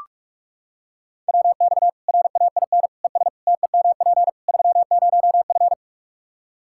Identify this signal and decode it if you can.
Morse code, heard as a keyed tone, about 45 wpm, 700 Hz. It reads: WX RAIN ES TEMP 30F
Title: K1USN SST QSO Practice 45wpm